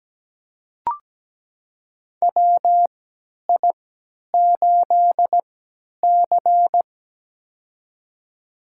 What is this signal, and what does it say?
Morse code, heard as a keyed tone, about 17 wpm, spaced as for 10 wpm, 700 Hz. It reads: WI8C
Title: USA Call Signs - 4 Characters - 3x Character Spacing - 17wpm